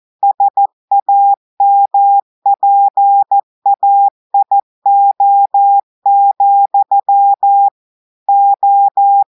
Morse code, heard as a keyed tone, about 14 wpm, 800 Hz. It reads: SAMPAIO, O
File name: audios / portugais / portugais texte en clair 1 vitesse 14 mots minutes